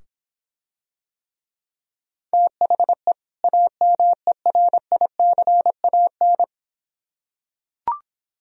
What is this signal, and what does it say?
Morse code, heard as a keyed tone, about 26 wpm, 700 Hz. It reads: THE AMERICAN